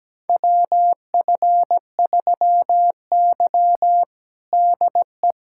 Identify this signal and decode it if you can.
Morse code, heard as a keyed tone, about 17 wpm, 700 Hz. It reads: WF3Y DE
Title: USA Callsign DE Callsign - 4 Characters - 17wpm